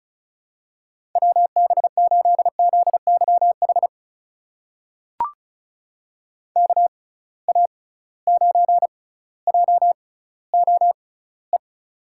Morse code, heard as a keyed tone, about 34 wpm, spaced as for 18 wpm, 700 Hz. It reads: WB8ZYH KA9JOE
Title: USA Call Signs - 6 Characters - 6x Character Spacing - 35wpm